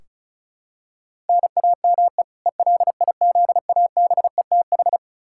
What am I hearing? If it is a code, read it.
Morse code, heard as a keyed tone, about 36 wpm, 700 Hz. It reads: NAME ELIZABETH
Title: K1USN SST QSO Practice 35wpm